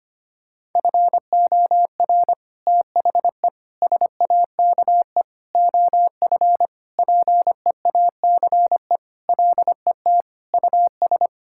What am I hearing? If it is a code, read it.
Morse code, heard as a keyed tone, about 25 wpm, 700 Hz. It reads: FOR THE SAKE OF PEACE LET US
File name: practice / 25wpm / Sentences with 4-Word Phrases - 25wpm